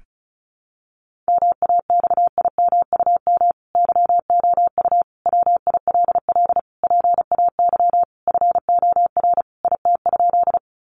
Morse code, heard as a keyed tone, about 34 wpm, 700 Hz. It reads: MAXIMUM YOU WILL PAY FOR IT?
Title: Sentences from Top 1300 Words 35wpm